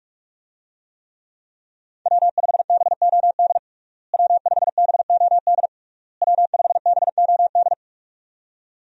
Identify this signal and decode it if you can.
Morse code, heard as a keyed tone, about 47 wpm, 700 Hz. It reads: W5BOD W5BOD W5BOD
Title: USA Call Signs - 5 Characters - Repeated 3x - 45wpm